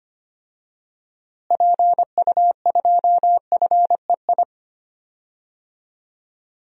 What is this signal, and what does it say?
Morse code, heard as a keyed tone, about 25 wpm, 700 Hz. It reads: PU2FEI
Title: DXCC-100 Call Signs - 6 Characters - Repeated 3x - 25wpm